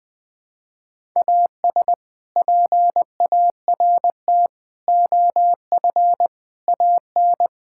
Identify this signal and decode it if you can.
Morse code, heard as a keyed tone, about 21 wpm, 700 Hz. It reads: AS PART OF AN
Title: Sets of 6 Words - 20wpm